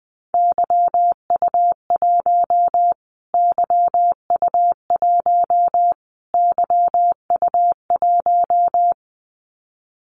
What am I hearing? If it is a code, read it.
Morse code, heard as a keyed tone, about 20 wpm, 700 Hz. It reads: YU1 YU1 YU1
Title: Callsign Prefixes for Top 25 DXCC - Repeated 3x - 20wpm